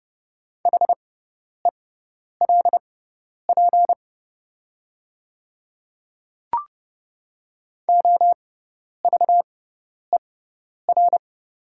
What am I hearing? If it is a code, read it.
Morse code, heard as a keyed tone, about 30 wpm, spaced as for 10 wpm, 690 Hz. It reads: HELP OVER
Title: 6x Character Spacing - Top 300 Words - 30wpm